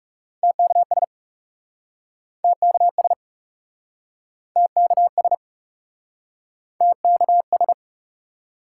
Morse code, heard as a keyed tone, about 41 wpm, 700 Hz. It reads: TKS TKS TKS TKS